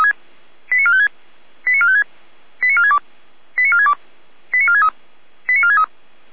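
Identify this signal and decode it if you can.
Five-tone selective call (ZVEI), heard as quick runs of five tones, tones of about 70 ms. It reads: 89456 89456 89462 89462 89463 89463
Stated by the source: ZVEI five-tone call received by SDR.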